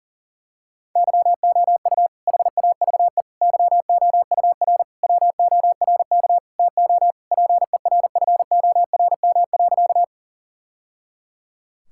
Morse code, heard as a keyed tone, about 40 wpm, 700 Hz. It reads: YOU HAVE YOUR WORK TO PERFORM.
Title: Sentences from Top 1000 Words - Review Part 5 - 40wpm